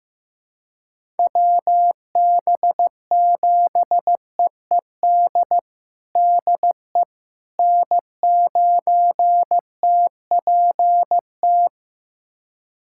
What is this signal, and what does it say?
Morse code, heard as a keyed tone, about 15 wpm, 700 Hz. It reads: WB7EED DE N9TPT
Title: USA Callsign DE Callsign 15wpm